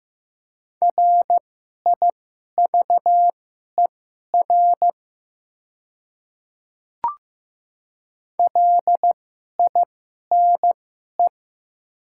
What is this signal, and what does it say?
Morse code, heard as a keyed tone, about 15 wpm, spaced as for 11 wpm, 700 Hz. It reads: RIVER LINE
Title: 2x Character Spacing - Top 500 Words - 15wpm